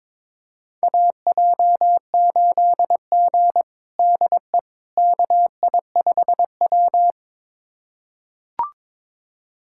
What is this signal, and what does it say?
Morse code, heard as a keyed tone, about 22 wpm, 700 Hz. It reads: AJ8G DE KI5W
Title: USA Callsign DE Callsign - 4 Characters - 22wpm